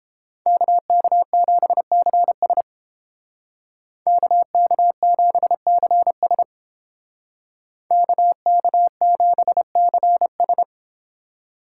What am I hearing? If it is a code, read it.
Morse code, heard as a keyed tone, about 30 wpm, 700 Hz. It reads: KK7CS KK7CS KK7CS